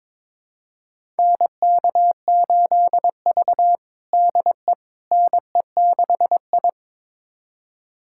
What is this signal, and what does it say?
Morse code, heard as a keyed tone, about 22 wpm, 700 Hz. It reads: NK8V DE NE6I
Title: USA Callsign DE Callsign - 4 Characters - 22wpm